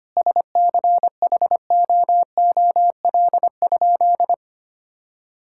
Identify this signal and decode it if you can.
Morse code, heard as a keyed tone, about 25 wpm, 700 Hz. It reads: SCHOOL?
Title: Sentences from Top 1000 Words - Review Part 6 - 25wpm